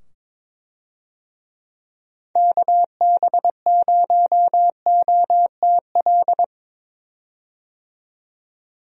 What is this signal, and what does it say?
Morse code, heard as a keyed tone, about 22 wpm, 700 Hz. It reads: KB0OTL
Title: USA Call Signs - 6 Characters - 22wpm